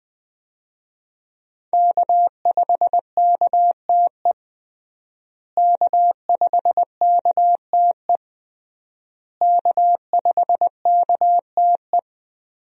Morse code, heard as a keyed tone, about 20 wpm, 700 Hz. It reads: K5KTE K5KTE K5KTE
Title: USA Call Signs - 5 Characters - Repeated 3x - 20wpm